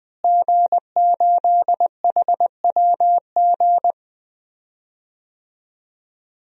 Morse code, heard as a keyed tone, about 20 wpm, 700 Hz. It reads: G8HWG
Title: DXCC-100 Call Signs - 5 Characters - Repeated 3x - 20wpm